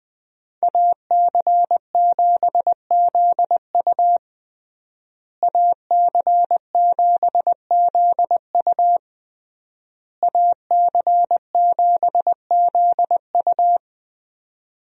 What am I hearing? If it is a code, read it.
Morse code, heard as a keyed tone, about 20 wpm, 700 Hz. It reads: AC7ZU AC7ZU AC7ZU
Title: USA Call Signs - 5 Characters - Repeated 3x - 20wpm